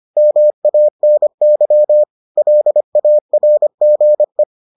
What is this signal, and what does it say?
Morse code, heard as a keyed tone, about 25 wpm, 600 Hz. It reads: MANY LARGE